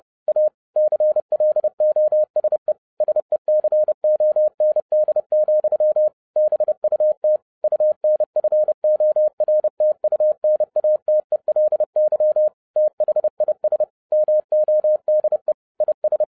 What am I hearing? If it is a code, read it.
Morse code, heard as a keyed tone, about 30 wpm, 610 Hz. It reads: A CLOSE SECOND, BUT UNFORTUNATELY THIS MODE IS